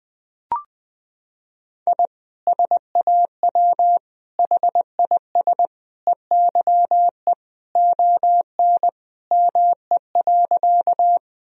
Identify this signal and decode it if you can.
Morse code, heard as a keyed tone, about 20 wpm, 700 Hz. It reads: I SAW HIS EYE ON ME.
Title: Sentences from Top 1000 Words - Review Part 1 - 20wpm